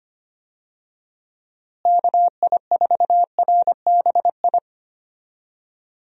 Morse code, heard as a keyed tone, about 25 wpm, 700 Hz. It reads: KI4RBI
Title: USA Call Signs - 6 Characters - Repeated 2x - 25wpm